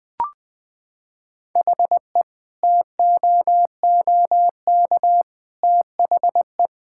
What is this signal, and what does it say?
Morse code, heard as a keyed tone, about 21 wpm, 700 Hz. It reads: HE TOOK THE
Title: Sentences with 3-Word Phrases - 20wpm